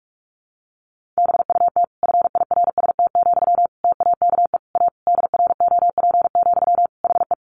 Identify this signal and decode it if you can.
Morse code, heard as a keyed tone, about 44 wpm, 700 Hz. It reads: BUT FIRST, TAKE A DROP, HE
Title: Sentences from Top 1400 Words 45wpm